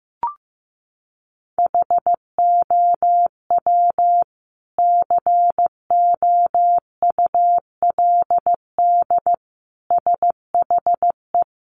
Morse code, heard as a keyed tone, about 15 wpm, 700 Hz. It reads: HOW COULD SHE
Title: Sentences from Top 200 Words 15wpm